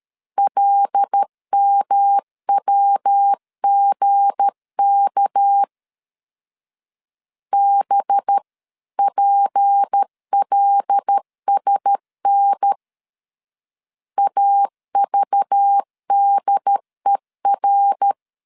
Morse code, heard as a keyed tone, about 13 wpm, 800 Hz. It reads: LMWGK BPLSN AVDER
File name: LEZIONI / 8 YJ45 / 8 Y 12wpm 2228